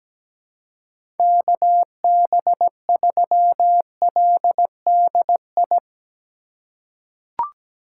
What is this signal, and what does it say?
Morse code, heard as a keyed tone, about 17 wpm, 700 Hz. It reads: KB3LDI
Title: USA Call Signs - 6 Characters - 4x Character Spacing - 17wpm